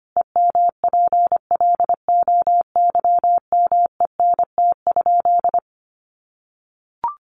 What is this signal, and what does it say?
Morse code, heard as a keyed tone, about 25 wpm, 700 Hz. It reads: EMPLOYMENT?
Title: Sentences from Top 1500 Words 25wpm